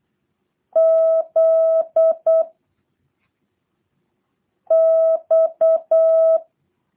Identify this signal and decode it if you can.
Morse code, heard as a keyed tone, about 8 wpm, 650 Hz. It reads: Z X